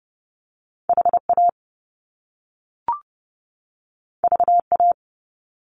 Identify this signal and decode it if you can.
Morse code, heard as a keyed tone, about 30 wpm, 700 Hz. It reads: HA VA
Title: Top Two Letter Combinations in Words 30wpm